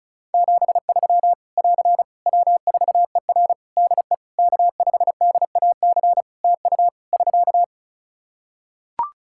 Morse code, heard as a keyed tone, about 36 wpm, 700 Hz. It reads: 73 <AR> W4ER DE K5DAC TU <SK>
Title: K1USN SST QSO Practice 35wpm